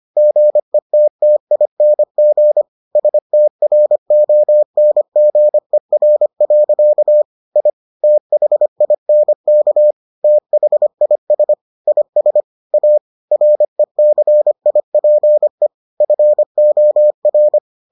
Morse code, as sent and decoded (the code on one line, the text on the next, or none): --. - - .. -. --. / ... - .-. --- -. --. .-. .-.-.- / .. / - .... .. -. -.- / - .... .. ... / .. ... / .- / .-. -.-. .. .--. / ..-. --- .-.
GETTING STRONGER. I THINK THIS IS A RECIPE FOR